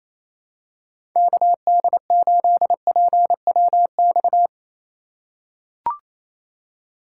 Morse code, heard as a keyed tone, about 28 wpm, 700 Hz. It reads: KD8PWX
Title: USA Call Signs - 6 Characters - 4x Character Spacing - 28wpm